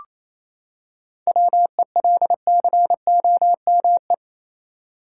Morse code, heard as a keyed tone, about 28 wpm, 700 Hz. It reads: WELCOME
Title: Seven Letter Words - 28wpm